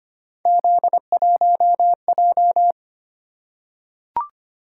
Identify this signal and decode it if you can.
Morse code, heard as a keyed tone, about 25 wpm, 700 Hz. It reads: Z1J
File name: Three Letter-Number Combinations - 25wpm